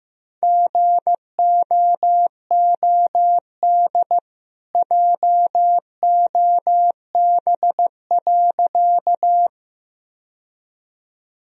Morse code, heard as a keyed tone, about 16 wpm, 700 Hz. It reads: GOOD JOB.
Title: Sentences from Top 1000 Words - Review Part 3 - 15wpm